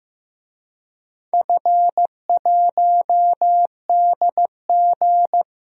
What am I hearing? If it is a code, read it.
Morse code, heard as a keyed tone, about 15 wpm, 700 Hz. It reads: F1DG
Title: DXCC-100 Call Signs - 4 Characters - 15wpm